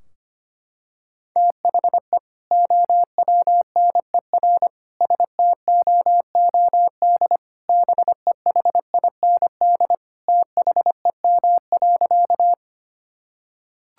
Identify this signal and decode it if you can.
Morse code, heard as a keyed tone, about 25 wpm, 700 Hz. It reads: THE OWNER STOOD BEHIND THEM.
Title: Sentences from Top 1600 Words 25wpm